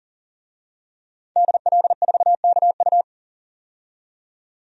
Morse code, heard as a keyed tone, about 40 wpm, 700 Hz. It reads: DL4KU
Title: DXCC-100 Call Signs - 5 Characters - 40wpm